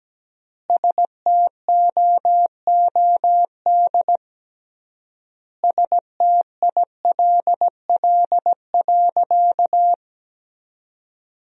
Morse code, heard as a keyed tone, about 17 wpm, 700 Hz. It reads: STOOD STILL.
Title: 3x Word-Spacing - Sentences from Top 500 Words - 17wpm